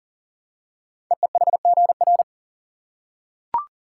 Morse code, heard as a keyed tone, about 39 wpm, 700 Hz. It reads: EEHGR
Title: Code Groups - 5 Letters - 40wpm